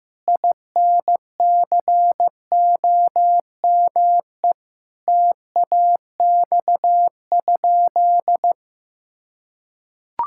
Morse code, heard as a keyed tone, about 15 wpm, 700 Hz. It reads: INCOME TAX?
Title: Sentences from Top 1000 Words - Review Part 5 - 15wpm